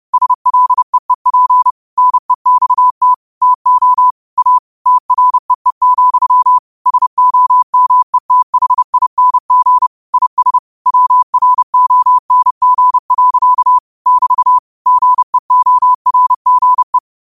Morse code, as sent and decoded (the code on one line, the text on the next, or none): ... .-.. . .--. / -. -..- - / - --- / .- / - .-. . --..-- / ... --- -- . - .... .. -. --. / .. ... / .-- .-. --- -. --. .-.-.- / -...- / --. --- .-. --. .
SLEEP NEXT TO A TREE, SOMETHING IS WRONG. <BT> GEORGE